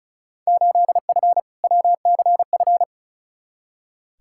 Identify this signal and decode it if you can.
Morse code, heard as a keyed tone, about 36 wpm, 700 Hz. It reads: 8F WCF
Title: Field Day Exchanges Encore - 35wpm